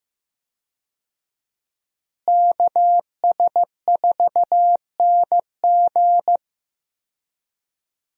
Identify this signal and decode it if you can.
Morse code, heard as a keyed tone, about 15 wpm, 700 Hz. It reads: KS4NG
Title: USA Call Signs - 5 Characters - Repeated 2x - 15wpm